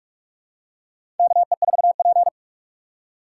Morse code, heard as a keyed tone, about 46 wpm, 700 Hz. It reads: KE4P
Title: USA Call Signs - 4x Character Spacing - 45wpm